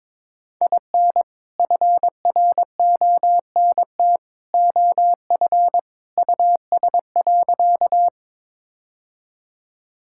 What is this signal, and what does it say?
Morse code, heard as a keyed tone, about 22 wpm, 700 Hz. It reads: IN FRONT OF US.
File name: practice / 22wpm / Sentences with 2-Word Phrases - 22wpm